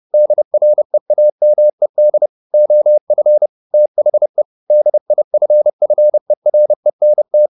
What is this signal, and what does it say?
Morse code, heard as a keyed tone, about 30 wpm, 600 Hz. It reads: DREAMED OF THE DIFFERENT
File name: the old man and the sea / the old man and the sea 1 30wpm